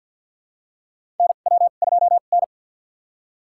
Morse code, heard as a keyed tone, about 52 wpm, spaced as for 37 wpm, 690 Hz. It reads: NW2N
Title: USA Call Signs - 2x Character Spacing - 50wpm